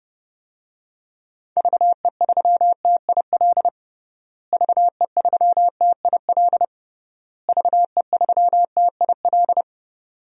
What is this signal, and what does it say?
Morse code, heard as a keyed tone, about 32 wpm, 700 Hz. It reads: VE3TIL VE3TIL VE3TIL
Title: DXCC-100 Call Signs - 6 Characters - Repeated 3x - 30wpm